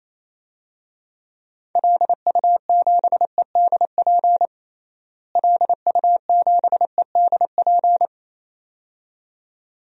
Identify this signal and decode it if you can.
Morse code, heard as a keyed tone, about 28 wpm, 700 Hz. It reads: LU7EDP LU7EDP
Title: DXCC-100 Call Signs - 6 Characters - Repeated 2x - 28wpm